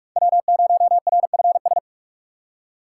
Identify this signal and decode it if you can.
Morse code, heard as a keyed tone, about 46 wpm, 700 Hz. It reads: W0RFS
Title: USA Call Signs - 5 Characters - 4x Character Spacing - 45wpm